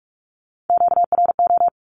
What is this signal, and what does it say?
Morse code, heard as a keyed tone, about 44 wpm, 700 Hz. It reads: QRO
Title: Q-Codes Repeated 3x 45wpm